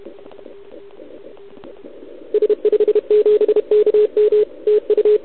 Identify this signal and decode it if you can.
Morse code, heard as a keyed tone, about 31 wpm, 440 Hz. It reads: S57KM TU
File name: ARCHIV / OKDX2019 / OKOM40 KX3